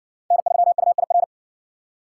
Morse code, heard as a keyed tone, about 60 wpm, 700 Hz. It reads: N3UIR